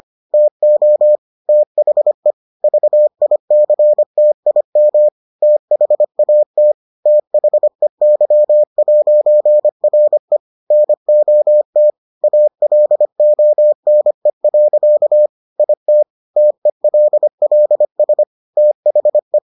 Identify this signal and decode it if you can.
Morse code, heard as a keyed tone, about 25 wpm, 600 Hz. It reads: TO THE VICTIM THAT THEY'RE NOT ALONE. IT TELLS THE